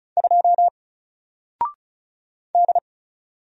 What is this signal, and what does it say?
Morse code, heard as a keyed tone, about 36 wpm, 700 Hz. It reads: 2 D
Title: Single Character (Letter:Number:Punctuation:Prosign) Alt 35wpm